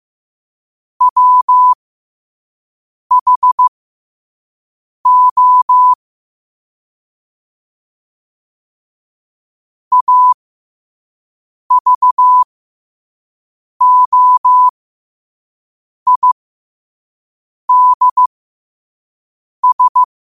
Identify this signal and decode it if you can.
Morse code, heard as a keyed tone, about 15 wpm, spaced as for 5 wpm, 1000 Hz. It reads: WHO AVOIDS
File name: qotdfiles / 5wpm